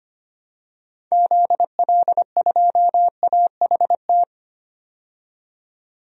Morse code, heard as a keyed tone, about 25 wpm, 700 Hz. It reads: ZL2AHT
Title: DXCC-100 Call Signs - 6 Characters - 25wpm